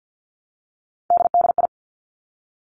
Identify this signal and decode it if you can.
Morse code, heard as a keyed tone, about 50 wpm, 700 Hz. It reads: DDI
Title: Three Challenging Letter-Number Combinations 50wpm